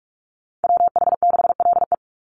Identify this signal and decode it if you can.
Morse code, heard as a keyed tone, about 44 wpm, 700 Hz. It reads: WH6LE
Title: NA SOTA Chaser Callsigns 45wpm